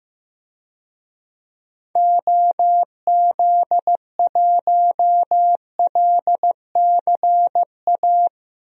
Morse code, heard as a keyed tone, about 15 wpm, 700 Hz. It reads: OZ1LCA
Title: DXCC-100 Call Signs - 6 Characters - Repeated 2x - 15wpm